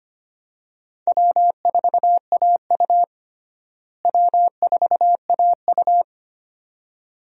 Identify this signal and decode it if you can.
Morse code, heard as a keyed tone, about 25 wpm, 700 Hz. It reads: W4AU W4AU
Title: USA Call Signs - 4 Characters - Repeated 2x - 25wpm